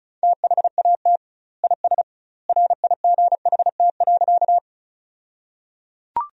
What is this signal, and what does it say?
Morse code, heard as a keyed tone, about 36 wpm, 700 Hz. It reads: THAT IS RIGHT.
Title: 2x Word-Spacing - Sentences from Top 200 Words - 35wpm